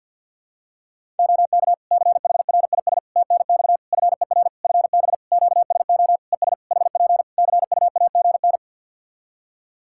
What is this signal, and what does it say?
Morse code, heard as a keyed tone, about 48 wpm, 700 Hz. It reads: OK CHRIS TNX FER FB QSO ES HP CUAGN
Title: Practice QSO - Exchange 4 - Ending - 50wpm